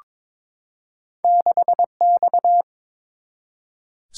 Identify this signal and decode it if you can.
Morse code, heard as a keyed tone, about 22 wpm, 700 Hz. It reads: 6X
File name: practice / 22wpm / Two Letter-Number Combinations - 22wpm